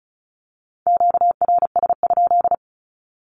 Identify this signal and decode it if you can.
Morse code, heard as a keyed tone, about 34 wpm, 700 Hz. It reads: QRS?
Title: Q-Codes 35wpm